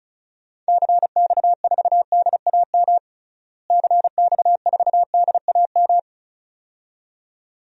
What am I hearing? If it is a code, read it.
Morse code, heard as a keyed tone, about 34 wpm, 700 Hz. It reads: CX4DAM CX4DAM
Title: DXCC-100 Call Signs - 6 Characters - Repeated 2x - 35wpm